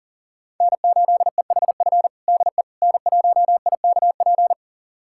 Morse code, heard as a keyed tone, about 37 wpm, 690 Hz. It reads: N8EHF DE N1IKP
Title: USA Callsign DE Callsign - 5 Characters - 40wpm